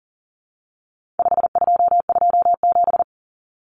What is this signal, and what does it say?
Morse code, heard as a keyed tone, about 40 wpm, 700 Hz. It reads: 5227
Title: Common Cut Numbers - 4-Digit 40wpm